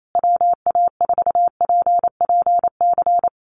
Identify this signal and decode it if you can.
Morse code, heard as a keyed tone, about 28 wpm, 700 Hz. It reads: WA4PPC
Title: USA Call Signs Repeated 2x 28wpm